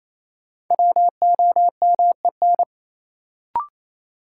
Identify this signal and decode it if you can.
Morse code, heard as a keyed tone, about 28 wpm, 700 Hz. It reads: WOMEN
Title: Sets of 3 Words - New Word Drill - 28wpm